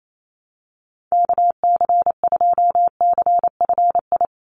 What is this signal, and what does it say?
Morse code, heard as a keyed tone, about 28 wpm, 700 Hz. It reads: KC2CFI
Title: USA Call Signs 28wpm